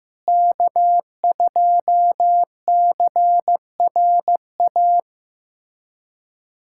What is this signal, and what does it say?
Morse code, heard as a keyed tone, about 15 wpm, 700 Hz. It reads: K2CRA
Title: USA Callsign DE Callsign - 5 Characters - 15wpm